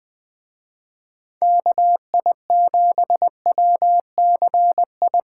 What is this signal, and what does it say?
Morse code, heard as a keyed tone, about 25 wpm, 700 Hz. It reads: KI7WCI
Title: Speed Racing - USA 2x3 Call Signs 20wpm